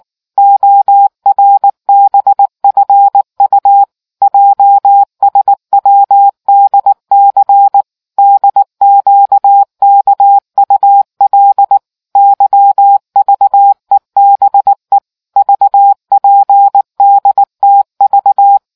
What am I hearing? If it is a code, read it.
Morse code, heard as a keyed tone, about 19 wpm, 800 Hz. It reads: ORBFU JSWDC DQKUL YVEBE VPDTV